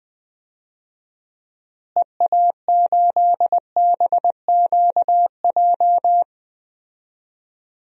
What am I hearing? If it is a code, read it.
Morse code, heard as a keyed tone, about 20 wpm, 700 Hz. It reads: EA8BQJ
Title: DXCC-100 Call Signs - 6 Characters - Repeated 2x - 20wpm